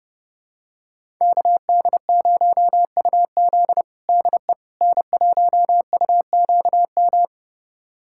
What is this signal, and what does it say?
Morse code, heard as a keyed tone, about 30 wpm, 700 Hz. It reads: KD0UZ DE N1UQM
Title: USA Callsign DE Callsign - 5 Characters - 30wpm